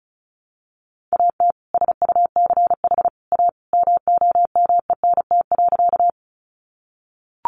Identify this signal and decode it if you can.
Morse code, heard as a keyed tone, about 34 wpm, 700 Hz. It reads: AT SUCH A MOMENT.
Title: Sentences from Top 1000 Words 35wpm